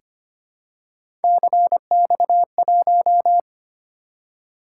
Morse code, heard as a keyed tone, about 25 wpm, 700 Hz. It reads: CX1
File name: Three Letter-Number Combinations - 25wpm